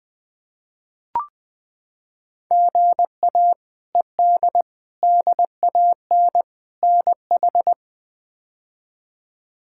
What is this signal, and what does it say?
Morse code, heard as a keyed tone, about 20 wpm, 700 Hz. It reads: GA ED DAN NH
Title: K1USN SST QSO Practice - 20wpm